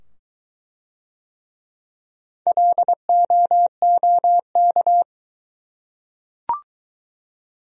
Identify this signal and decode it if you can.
Morse code, heard as a keyed tone, about 23 wpm, 700 Hz. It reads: LOOK